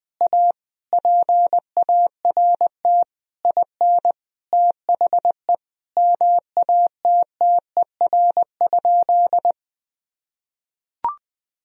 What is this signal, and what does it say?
Morse code, heard as a keyed tone, about 21 wpm, 700 Hz. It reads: A PART IN THE MATTER?
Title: Sentences from Top 1000 Words - Review Part 1 - 20wpm